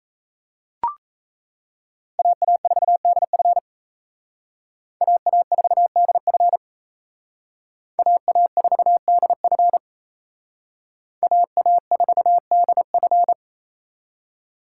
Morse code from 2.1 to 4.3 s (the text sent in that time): AA4DF